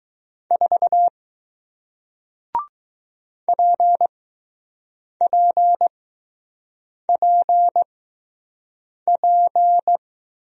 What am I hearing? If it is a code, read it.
Morse code, heard as a keyed tone, about 20 wpm, spaced as for 6 wpm, 700 Hz. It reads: 4 PPPP